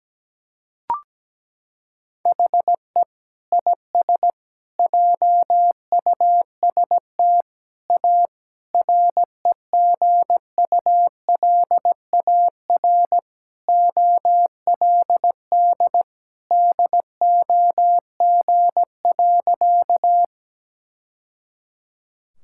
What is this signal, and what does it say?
Morse code, heard as a keyed tone, about 17 wpm, 700 Hz. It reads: HE IS JUST A REGULAR OLD DOG.
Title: Sentences from Top 1000 Words - Review Part 6 - 17wpm